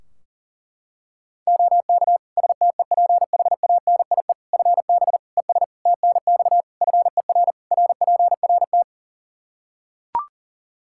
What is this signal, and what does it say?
Morse code, heard as a keyed tone, about 40 wpm, 700 Hz. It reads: OK STEPHANIE FB ES TNX FER RPRT
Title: Practice QSO - Exchange 2 - Info - 40wpm